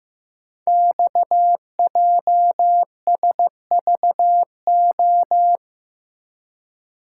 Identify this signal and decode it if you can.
Morse code, heard as a keyed tone, about 15 wpm, 700 Hz. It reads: XJSVO
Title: Code Groups - 5 Letters - 15wpm